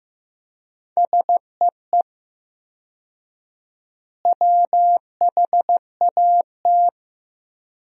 Morse code, heard as a keyed tone, about 15 wpm, 700 Hz. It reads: SEE WHAT
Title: 4x Word-Spacing - Sentences from Top 500 Words - 15wpm